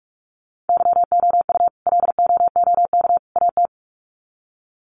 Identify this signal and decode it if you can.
Morse code, heard as a keyed tone, about 44 wpm, 700 Hz. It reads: YOU LOOK AT
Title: Sets of 3 Words 45wpm